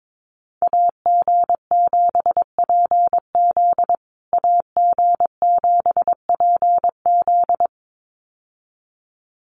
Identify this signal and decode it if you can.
Morse code, heard as a keyed tone, about 22 wpm, 700 Hz. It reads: AG7PZ AG7PZ
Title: USA Call Signs Repeated 2x 22wpm